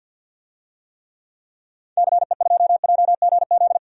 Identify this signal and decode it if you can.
Morse code, heard as a keyed tone, about 52 wpm, 700 Hz. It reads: CE2JGZ
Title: DXCC-100 Call Signs - 6 Characters - 50wpm